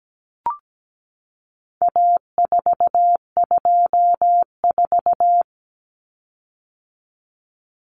Morse code, heard as a keyed tone, about 17 wpm, 700 Hz. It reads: A424
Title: Common Cut Numbers - 4-Digit 17wpm